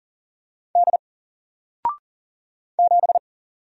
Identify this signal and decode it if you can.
Morse code, heard as a keyed tone, about 39 wpm, 700 Hz. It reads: D 7
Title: Single Letter-Number Alt 40wpm